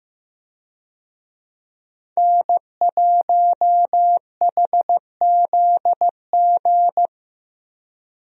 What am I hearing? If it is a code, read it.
Morse code, heard as a keyed tone, about 16 wpm, 700 Hz. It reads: N1HZG
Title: USA Call Signs - 5 Characters - Repeated 3x - 15wpm